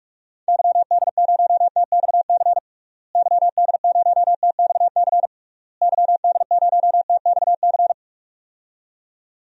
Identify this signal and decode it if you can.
Morse code, heard as a keyed tone, about 41 wpm, 700 Hz. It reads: YD0TXC YD0TXC YD0TXC
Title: DXCC-100 Call Signs - 6 Characters - Repeated 3x - 45wpm